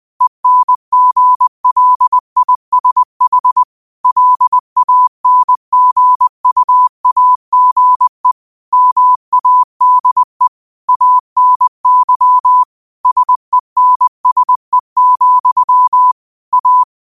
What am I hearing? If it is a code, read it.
Morse code, heard as a keyed tone, about 20 wpm, 1000 Hz. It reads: ENGLISH LANGUAGE MADE ANY SENSE, A
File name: qotdfiles / 20wpm